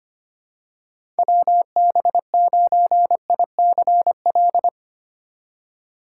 Morse code, heard as a keyed tone, about 25 wpm, 690 Hz. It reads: WB9ICL
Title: USA Call Signs - 6 Characters - 2x Character Spacing - 25wpm